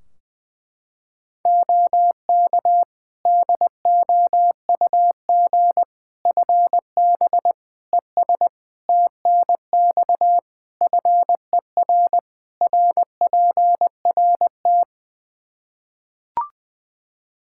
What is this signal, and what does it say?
Morse code, heard as a keyed tone, about 20 wpm, 700 Hz. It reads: OK DOUG FB ES TNX FER RPRT
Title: Practice QSO - Exchange 2 - Info - 20wpm